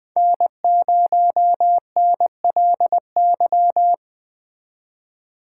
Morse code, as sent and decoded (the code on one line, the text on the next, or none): -. ----- -. .-.. -.--
N0NLY